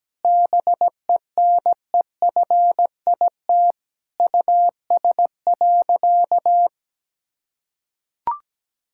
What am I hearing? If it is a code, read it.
Morse code, heard as a keyed tone, about 17 wpm, 700 Hz. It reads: BENEFIT US.
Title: Sentences from Top 1000 Words - Review Part 6 - 17wpm